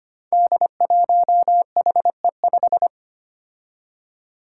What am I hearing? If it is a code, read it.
Morse code, heard as a keyed tone, about 25 wpm, 700 Hz. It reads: D1HE5
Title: Five Challenging Letter-Number Combinations 25wpm